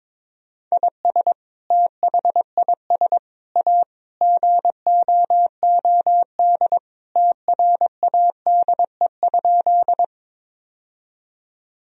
Morse code, heard as a keyed tone, about 22 wpm, 700 Hz. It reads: IS THIS A GOOD TRADE?
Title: Sentences from Top 1000 Words - Review Part 4 - 22wpm